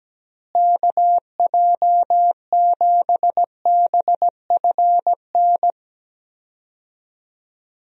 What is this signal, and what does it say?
Morse code, heard as a keyed tone, about 17 wpm, 700 Hz. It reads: KJ7BFN
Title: USA Call Signs - 6 Characters - Repeated 3x - 17wpm